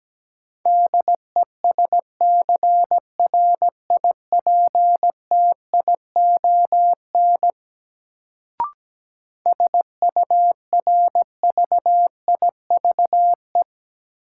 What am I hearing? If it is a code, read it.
Morse code, heard as a keyed tone, about 17 wpm, 700 Hz. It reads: DESCRIPTION SURVIVE
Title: Sentences from Top 1300 Words - New Word Drill Alt - 17wpm